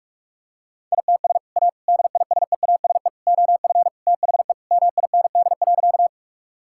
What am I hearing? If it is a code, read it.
Morse code, heard as a keyed tone, about 44 wpm, 700 Hz. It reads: ITS A DISEASE OF THE MIND.
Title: Sentences from Top 1000 Words - Review Part 6 - 45wpm